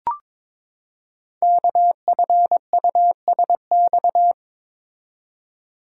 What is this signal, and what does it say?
Morse code, heard as a keyed tone, about 23 wpm, 700 Hz. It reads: KFUSX
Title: Code Groups - 5 Letters - 22wpm